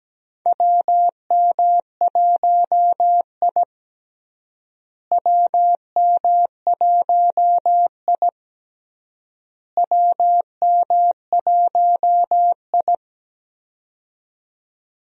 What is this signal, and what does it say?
Morse code, heard as a keyed tone, about 17 wpm, 700 Hz. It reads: WM1I WM1I WM1I
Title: USA Call Signs - 4 Characters - Repeated 3x - 17wpm